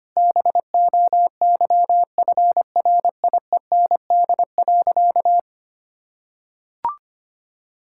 Morse code, heard as a keyed tone, about 25 wpm, 700 Hz. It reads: BOYFRIEND.
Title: Sentences from Top 1000 Words - Review Part 5 - 25wpm